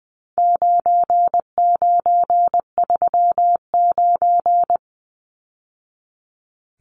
Morse code, heard as a keyed tone, about 20 wpm, 700 Hz. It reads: 9939
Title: Numbers - 4-Digit 20wpm